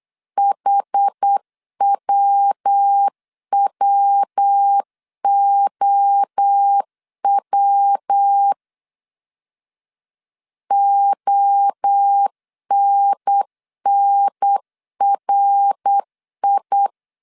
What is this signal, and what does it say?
Morse code, heard as a keyed tone, about 8 wpm, 800 Hz. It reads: HWWOW ONNRI
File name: CW / DET / 1 3 1500C S B 8wpm 4141